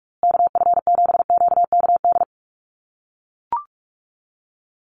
Morse code, heard as a keyed tone, about 44 wpm, 700 Hz. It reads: KF7QKD
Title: USA Call Signs Repeated 2x 45wpm